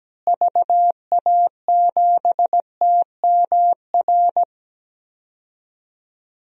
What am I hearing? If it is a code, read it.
Morse code, heard as a keyed tone, about 17 wpm, 700 Hz. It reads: VA7TMR
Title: DXCC-100 Call Signs - 6 Characters - Repeated 3x - 17wpm